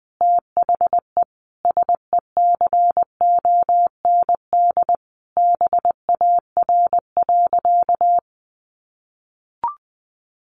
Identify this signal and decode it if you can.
Morse code, heard as a keyed tone, about 20 wpm, 700 Hz. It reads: THE SECOND BAR.
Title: Easy Phrases 20wpm